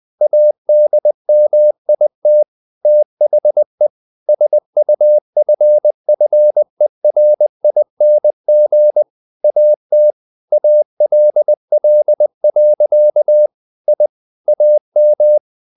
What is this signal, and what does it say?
Morse code, heard as a keyed tone, about 20 wpm, 600 Hz. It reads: ADMIT THE SUFFERING AT ALL. I AM
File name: the old man and the sea / the old man and the sea part 1 20wpm